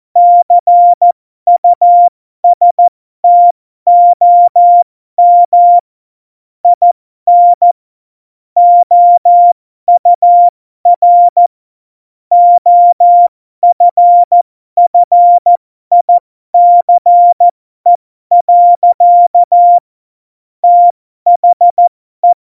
Morse code, heard as a keyed tone, about 14 wpm, spaced as for 12 wpm, 700 Hz. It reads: CUSTOM IN OUR OFFICE. THE
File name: Morsekurs / Witze